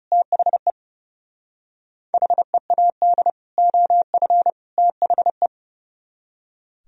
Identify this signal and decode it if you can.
Morse code, heard as a keyed tone, about 37 wpm, 700 Hz. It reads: THE HEAD OF THE
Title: Speed-Racing - Sets of 3 Words 30wpm